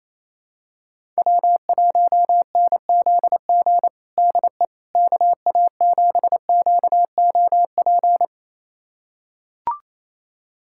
Morse code, heard as a keyed tone, about 27 wpm, 700 Hz. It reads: W1NZG DE KA7QOP
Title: USA Callsign DE Callsign 28wpm